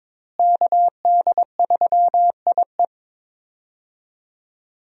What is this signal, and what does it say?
Morse code, heard as a keyed tone, about 22 wpm, 700 Hz. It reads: KD3IE
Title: USA Call Signs - 5 Characters - Repeated 3x - 22wpm